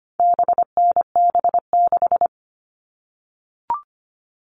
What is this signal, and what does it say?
Morse code, heard as a keyed tone, about 25 wpm, 700 Hz. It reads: BNB6
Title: All Cut Numbers - 4-Digit 25wpm